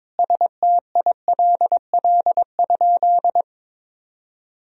Morse code, heard as a keyed tone, about 22 wpm, 700 Hz. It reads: STILL?
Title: Sentences from Top 1000 Words - Review Part 3 - 22wpm